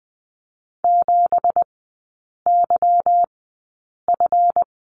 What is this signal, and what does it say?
Morse code, heard as a keyed tone, about 20 wpm, spaced as for 8 wpm, 700 Hz. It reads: 7YF
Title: 178 Lesson 29e - Slash - Three Characters 20wpm